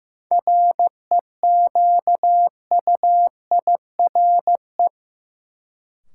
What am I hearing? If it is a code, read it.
Morse code, heard as a keyed tone, about 15 wpm, 700 Hz. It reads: REQUIRE
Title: Sentences from Top 900 Words - New Word Drill - 15wpm